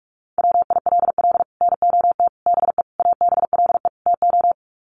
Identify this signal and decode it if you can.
Morse code, heard as a keyed tone, about 44 wpm, 700 Hz. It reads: WILL NOT BE ABLE TO
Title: Sets of 5 Words 45wpm